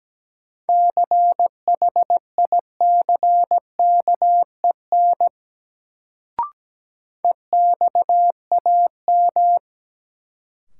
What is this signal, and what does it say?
Morse code, heard as a keyed tone, about 17 wpm, 700 Hz. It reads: CHICKEN EXAM
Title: Sentences from Top 1000 Words - Review Part 5 - New Word Drill - 17wpm